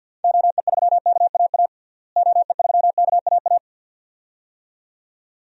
Morse code, heard as a keyed tone, about 52 wpm, 700 Hz. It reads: OE3KAA OE3KAA
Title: DXCC-100 Call Signs - 6 Characters - Repeated 2x - 50wpm